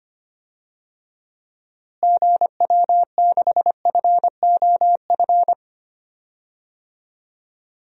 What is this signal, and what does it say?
Morse code, heard as a keyed tone, about 25 wpm, 700 Hz. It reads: GW6FOF
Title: DXCC-100 Call Signs - 6 Characters - Repeated 2x - 25wpm